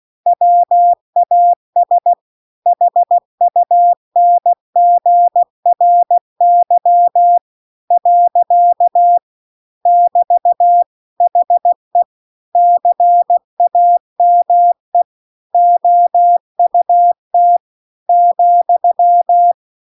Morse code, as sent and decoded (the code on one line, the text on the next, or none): .-- .- ... / .... ..- -. --. .-. -.-- / .-.-.- / -...- / .... / -.-. .- -- . / --- ..- - / --..--
WAS HUNGRY . <BT> HE CAME OUT ,